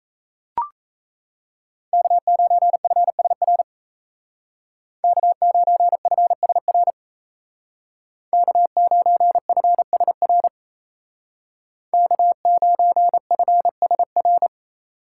Morse code, heard as a keyed tone, about 34 wpm, 700 Hz. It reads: K9FSR K9FSR K9FSR K9FSR
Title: Speed Racing - USA 1x3 Call Signs 28wpm